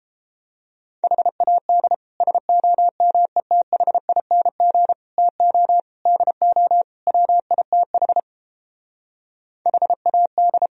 Morse code, heard as a keyed tone, about 29 wpm, 700 Hz. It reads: HAD SOMETHING TO DO WITH HAD